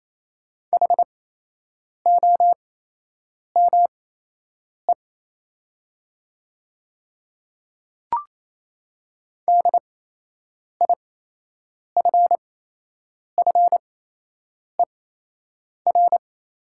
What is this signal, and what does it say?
Morse code, heard as a keyed tone, about 27 wpm, spaced as for 8 wpm, 700 Hz. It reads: HOME DIFFER
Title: 8x Character Spacing - Top 200 Words - 28wpm